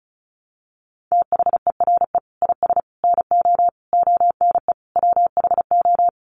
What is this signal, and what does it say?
Morse code, heard as a keyed tone, about 34 wpm, 700 Hz. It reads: THERE IS NO ONE WHO
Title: Sentences from Top 1100 Words 35wpm